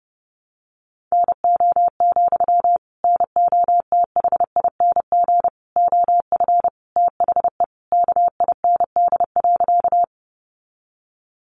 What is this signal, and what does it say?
Morse code, heard as a keyed tone, about 28 wpm, 700 Hz. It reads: NO, NOTHING OF THE KIND.
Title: Sentences from Top 500 Words 30wpm